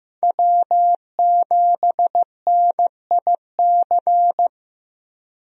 Morse code, heard as a keyed tone, about 15 wpm, 700 Hz. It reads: W7NIC
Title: USA Call Signs - 5 Characters - Repeated 3x - 15wpm